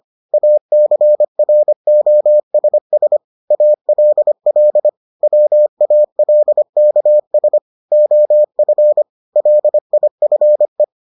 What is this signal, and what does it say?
Morse code, heard as a keyed tone, about 25 wpm, 600 Hz. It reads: ACROSS ALL WALKS OF LIFE